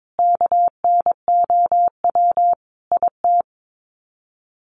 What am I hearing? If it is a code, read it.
Morse code, heard as a keyed tone, about 22 wpm, 700 Hz. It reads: KNOW IT
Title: Sets of 5 Words Encore2 22wpm